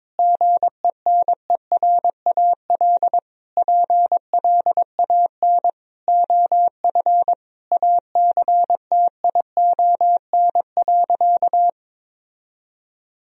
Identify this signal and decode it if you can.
Morse code, heard as a keyed tone, about 22 wpm, 700 Hz. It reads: GENERAL PLAN OF ACTION.
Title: Sentences from Top 1000 Words - Review Part 4 - 22wpm